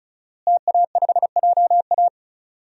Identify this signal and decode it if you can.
Morse code, heard as a keyed tone, about 36 wpm, 700 Hz. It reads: TA5JA
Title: DXCC-100 Call Signs - 5 Characters - Repeated 2x - 35wpm